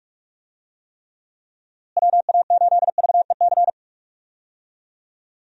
Morse code, heard as a keyed tone, about 42 wpm, 700 Hz. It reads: WA8VEC
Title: USA Call Signs - 6 Characters - Repeated 2x - 45wpm